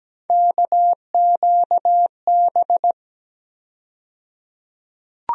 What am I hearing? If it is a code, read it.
Morse code, heard as a keyed tone, about 17 wpm, 700 Hz. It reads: KQB
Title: USA Call Sign Postfixes - 3 Letters 17wpm